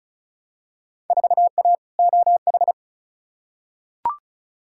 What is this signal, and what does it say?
Morse code, heard as a keyed tone, about 36 wpm, 700 Hz. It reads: 4A OH
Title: Field Day Exchanges - 35wpm